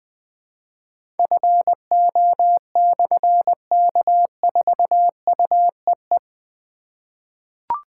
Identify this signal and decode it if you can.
Morse code, heard as a keyed tone, about 20 wpm, 700 Hz. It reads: FO/K4UEE
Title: Difficult Call Signs 20wpm